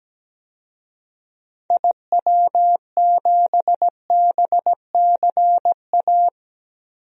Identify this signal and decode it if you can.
Morse code, heard as a keyed tone, about 17 wpm, 700 Hz. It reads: IW7BCA
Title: DXCC-100 Call Signs - 6 Characters - Repeated 2x - 17wpm